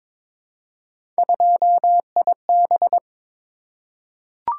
Morse code, heard as a keyed tone, about 22 wpm, 700 Hz. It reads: 2IB
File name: practice / 22wpm / Three Letter-Number Combinations - 22wpm